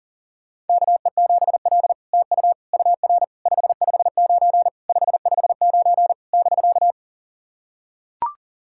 Morse code, heard as a keyed tone, about 37 wpm, 690 Hz. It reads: KE7L TU UR 559 559 <BK>
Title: SOTA QSO Practice 40wpm